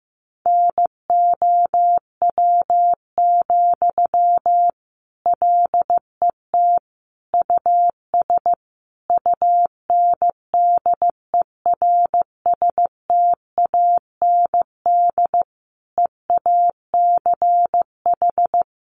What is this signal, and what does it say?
Morse code, heard as a keyed tone, about 15 wpm, 700 Hz. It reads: NOW, LET US UNDERSTAND EACH
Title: Sentences from Top 1500 Words 15wpm